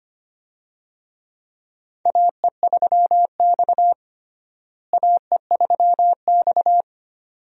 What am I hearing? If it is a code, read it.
Morse code, heard as a keyed tone, about 25 wpm, 700 Hz. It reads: AE3X AE3X
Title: USA Call Signs - 4 Characters - Repeated 2x - 25wpm